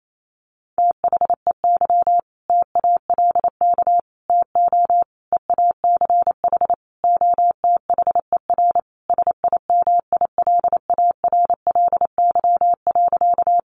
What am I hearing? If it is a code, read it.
Morse code, heard as a keyed tone, about 28 wpm, 700 Hz. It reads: THEY TALK TO EACH OTHER SIMILARLY.
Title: Sentences from Top 1400 Words 28wpm